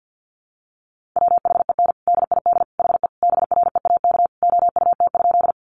Morse code, heard as a keyed tone, about 50 wpm, 700 Hz. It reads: WHEN DID HE BREAK OUT?
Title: Sentences from Top 900 Words 50wpm